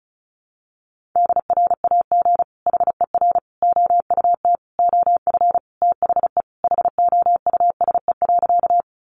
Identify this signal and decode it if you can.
Morse code, heard as a keyed tone, about 34 wpm, 700 Hz. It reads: DRAG HER OUT OF THE HOUSE.
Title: Sentences from Top 1500 Words 35wpm